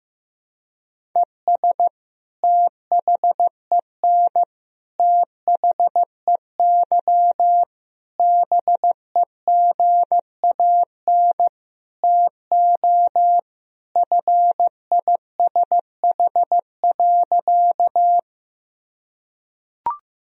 Morse code, heard as a keyed tone, about 15 wpm, 700 Hz. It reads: ES THEN THEY BEGAN TO FISH.
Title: Sentences from Top 300 Words and Abbreviations 15wpm